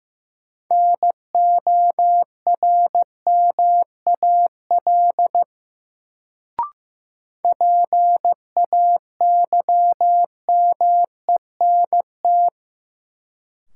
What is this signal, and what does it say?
Morse code, heard as a keyed tone, about 15 wpm, 700 Hz. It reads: NORMAL PAYMENT
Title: Sentences from Top 1200 Words - New Word Drill - 15wpm